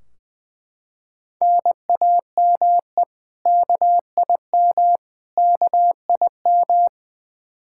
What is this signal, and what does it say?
Morse code, heard as a keyed tone, about 21 wpm, 700 Hz. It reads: NAME KIM KIM
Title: K1USN SST QSO Practice 20wpm